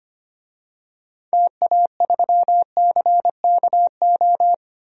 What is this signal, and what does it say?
Morse code, heard as a keyed tone, about 25 wpm, 690 Hz. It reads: TA3CKO
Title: DXCC-100 Call Signs - 6 Characters - 25wpm